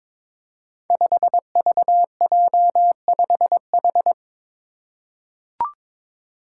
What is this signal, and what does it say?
Morse code, heard as a keyed tone, about 23 wpm, 700 Hz. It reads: 5VJ5H
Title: Five Challenging Letter-Number Combinations 22wpm